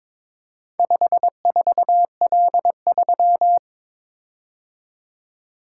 Morse code, heard as a keyed tone, about 22 wpm, 700 Hz. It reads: 54L3
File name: practice / 22wpm / Four Letter-Number Combinations - 22wpm